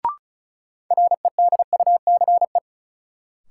Morse code, heard as a keyed tone, about 36 wpm, 700 Hz. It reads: REDUCE
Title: Sentences from Top 1100 Words - New Word Drill Alt - 35wpm